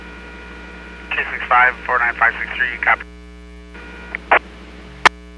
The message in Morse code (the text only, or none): none